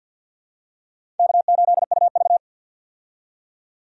Morse code, heard as a keyed tone, about 48 wpm, 700 Hz. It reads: K8UV
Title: USA Call Signs - 4 Characters - 50wpm